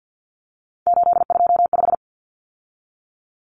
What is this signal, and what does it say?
Morse code, heard as a keyed tone, about 48 wpm, 700 Hz. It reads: 825
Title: Common Cut Numbers - 3-Digit 50wpm